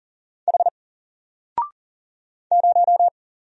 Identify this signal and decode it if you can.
Morse code, heard as a keyed tone, about 40 wpm, 700 Hz. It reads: H 0
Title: Single Character (Letter:Number:Punctuation:Prosign) 40wpm